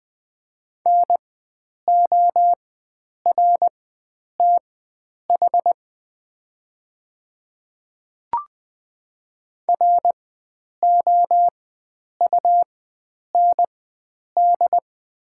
Morse code, heard as a keyed tone, about 20 wpm, spaced as for 9 wpm, 700 Hz. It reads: NORTH ROUND
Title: 4x Character Spacing - Top 500 Words - 20wpm